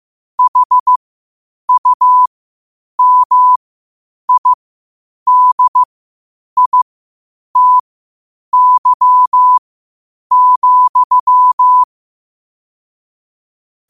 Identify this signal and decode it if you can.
Morse code, heard as a keyed tone, about 15 wpm, spaced as for 8 wpm, 1000 Hz. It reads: HUMIDITY,
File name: 8wpm